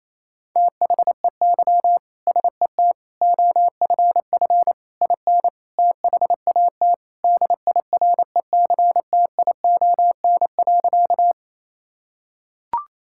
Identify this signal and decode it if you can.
Morse code, heard as a keyed tone, about 28 wpm, 700 Hz. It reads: THEY SET OFF IN THAT DIRECTION.
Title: Sentences from Top 1000 Words - Review Part 5 - 28wpm